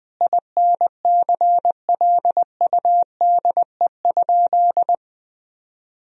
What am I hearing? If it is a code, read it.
Morse code, heard as a keyed tone, about 20 wpm, 700 Hz. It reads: INCLUDE?
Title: Sentences from Top 1000 Words - Review Part 3 - 20wpm